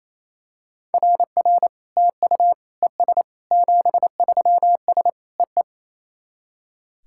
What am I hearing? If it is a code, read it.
Morse code, heard as a keyed tone, about 28 wpm, 700 Hz. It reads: RR TU ES 73S EE
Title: SOTA QSO Practice 28wpm